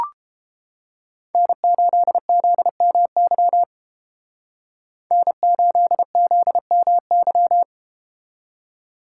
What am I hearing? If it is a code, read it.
Morse code, heard as a keyed tone, about 31 wpm, 700 Hz. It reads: N8ZMY N8ZMY